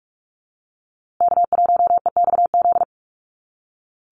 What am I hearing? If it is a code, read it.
Morse code, heard as a keyed tone, about 44 wpm, 690 Hz. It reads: K1EXZ
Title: USA Call Signs Repeated 2x 45wpm